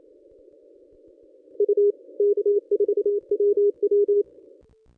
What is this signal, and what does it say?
Morse code, heard as a keyed tone, about 26 wpm, 410 Hz. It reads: U K4WW
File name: contests / ss-cw-k5tr-op-n5rz -1000